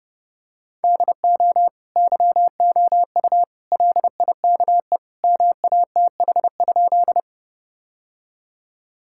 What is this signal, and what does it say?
Morse code, heard as a keyed tone, about 30 wpm, 700 Hz. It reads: DO YOU LIKE MATH?
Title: Sentences from Top 1000 Words - Review Part 6 - 30wpm